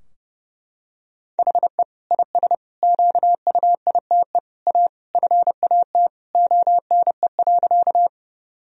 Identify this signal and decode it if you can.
Morse code, heard as a keyed tone, about 30 wpm, 700 Hz. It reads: HE IS QUITE A FAT ONE.
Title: Sentences from Top 1000 Words - Review Part 4 - 30wpm